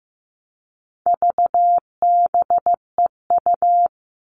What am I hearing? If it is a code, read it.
Morse code, heard as a keyed tone, about 15 wpm, 700 Hz. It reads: VBEU
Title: All Cut Numbers - 4-Digit 15wpm